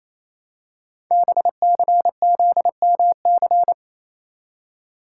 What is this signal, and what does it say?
Morse code, heard as a keyed tone, about 28 wpm, 700 Hz. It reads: BCZMC
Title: Code Groups - 5 Letters - 28wpm